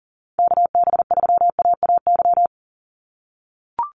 Morse code, heard as a keyed tone, about 37 wpm, 700 Hz. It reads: KB3AAY
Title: Call Signs - 40wpm